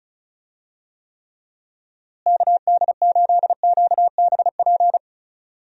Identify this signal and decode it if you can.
Morse code, heard as a keyed tone, about 34 wpm, 700 Hz. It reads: KD8QBP
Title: USA Call Signs - 6 Characters - Repeated 3x - 35wpm